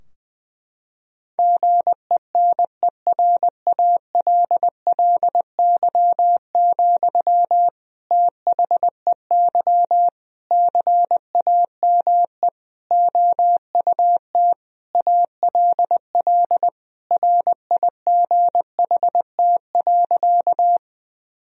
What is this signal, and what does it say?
Morse code, heard as a keyed tone, about 20 wpm, 700 Hz. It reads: GENERALLY, THEY CAME OUT ALL RIGHT.
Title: Sentences from Top 1000 Words - Review Part 4 - 20wpm